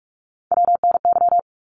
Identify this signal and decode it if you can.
Morse code, heard as a keyed tone, about 44 wpm, 700 Hz. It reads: WNY
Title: ARRL Sections 45wpm